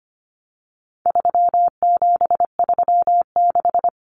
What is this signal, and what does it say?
Morse code, heard as a keyed tone, about 25 wpm, 700 Hz. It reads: 3736
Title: Numbers - 4-Digit 25wpm